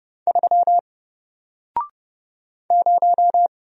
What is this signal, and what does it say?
Morse code, heard as a keyed tone, about 30 wpm, 700 Hz. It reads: 3 0
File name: Single-Letter-Number-30wpm